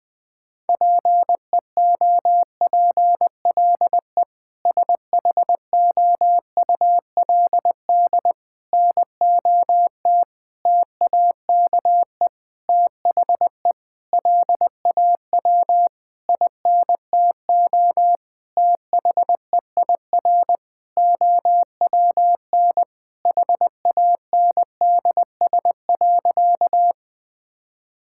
Morse code, heard as a keyed tone, about 20 wpm, 700 Hz. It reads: PEOPLE SHOULD NOT TAKE THE LAW INTO THEIR OWN HANDS.
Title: Sentences with 8-Word Phrases - 20wpm